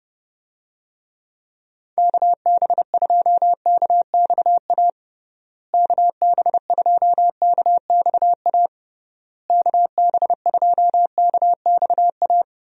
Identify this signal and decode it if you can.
Morse code, heard as a keyed tone, about 32 wpm, 700 Hz. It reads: KB2KXA KB2KXA KB2KXA
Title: USA Call Signs - 6 Characters - Repeated 3x - 30wpm